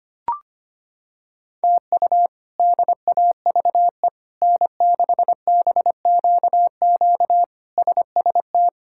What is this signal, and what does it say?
Morse code, heard as a keyed tone, about 25 wpm, 700 Hz. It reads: TU DAVE N6BQQ SST
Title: K1USN SST QSO Practice - 25wpm